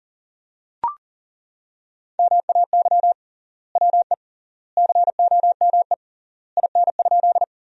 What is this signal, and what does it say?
Morse code, heard as a keyed tone, about 40 wpm, 700 Hz. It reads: MAY WE COME IN?
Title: 3x Word-Spacing - Sentences from Top 100 Words - 40wpm